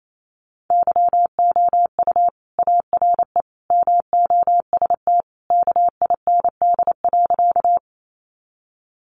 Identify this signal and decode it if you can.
Morse code, heard as a keyed tone, about 28 wpm, 700 Hz. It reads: YOU ARE MOST KIND.
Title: Sentences from Top 200 Words 28wpm